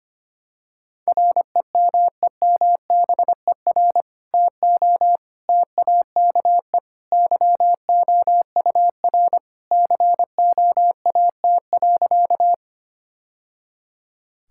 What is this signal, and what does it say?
Morse code, heard as a keyed tone, about 25 wpm, 700 Hz. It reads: REMEMBER TO TAKE YOUR COAT.
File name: practice / 25wpm / Sentences with 2-Word Phrases - 25wpm